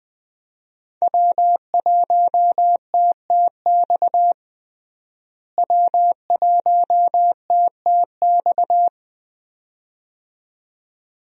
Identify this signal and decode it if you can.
Morse code, heard as a keyed tone, about 21 wpm, 700 Hz. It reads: W1TTX W1TTX
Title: USA Call Signs - 5 Characters - Repeated 2x - 20wpm